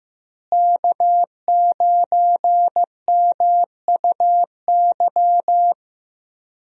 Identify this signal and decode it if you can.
Morse code, heard as a keyed tone, about 16 wpm, 700 Hz. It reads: K9MUY
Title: USA Call Signs - 5 Characters - Repeated 3x - 15wpm